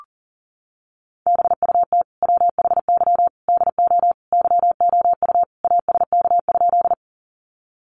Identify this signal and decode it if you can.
Morse code, heard as a keyed tone, about 40 wpm, 700 Hz. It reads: BUT WHY DO YOU ASK?
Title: Sentences from Top 200 Words 40wpm